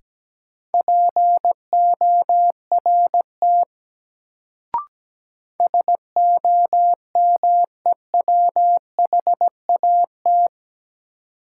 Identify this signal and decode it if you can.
Morse code, heard as a keyed tone, about 17 wpm, 700 Hz. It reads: PORT SOMEWHAT
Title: Sentences from Top 1000 Words - Review Part 1 - New Word Drill - 17wpm